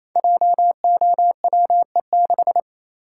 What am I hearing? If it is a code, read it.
Morse code, heard as a keyed tone, about 27 wpm, 700 Hz. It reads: JOWE6
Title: Code Groups - 5 Letter:Numbers - 28wpm